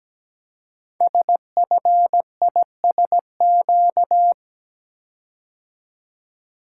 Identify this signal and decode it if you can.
Morse code, heard as a keyed tone, about 17 wpm, 700 Hz. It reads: SFISQ
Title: Code Groups - 5 Letter:Numbers - 17wpm